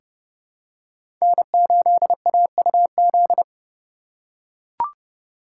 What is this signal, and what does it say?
Morse code, heard as a keyed tone, about 30 wpm, 700 Hz. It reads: N8AUZ
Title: USA Call Signs - 5 Characters - 3x Character Spacing - 30wpm